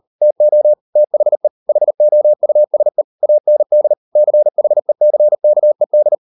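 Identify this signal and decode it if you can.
Morse code, heard as a keyed tone, about 38 wpm, 600 Hz. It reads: TO THE HOUSE AND CHECKED
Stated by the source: Morse code practice at 38 WPM, sending Ken Follett's novel Never in English, from CW med Gnister.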